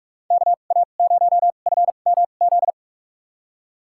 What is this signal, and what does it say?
Morse code, heard as a keyed tone, about 46 wpm, spaced as for 32 wpm, 700 Hz. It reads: KA0FMZ
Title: USA Call Signs - 6 Characters - 2x Character Spacing - 45wpm